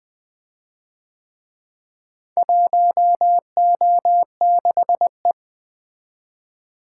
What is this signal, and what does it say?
Morse code, heard as a keyed tone, about 25 wpm, 700 Hz. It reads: 1O6E
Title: Speed-Racing - Four Letter-Number Combinations - 20wpm